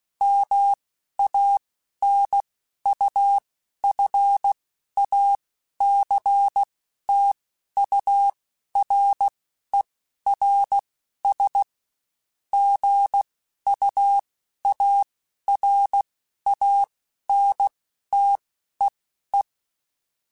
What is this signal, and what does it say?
Morse code, heard as a keyed tone, about 16 wpm, spaced as for 11 wpm, 800 Hz. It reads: MANUFACTURERS GUARANTEE
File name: Quotes Spaced / Quotes Spaced-mp3s / Airtight sp @15